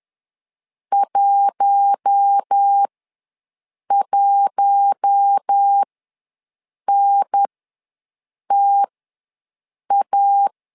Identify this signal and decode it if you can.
Morse code, heard as a keyed tone, about 10 wpm, spaced as for 6 wpm, 800 Hz. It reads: 11NTA
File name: LEZIONI / 4 Q1 ws 10wpm 2538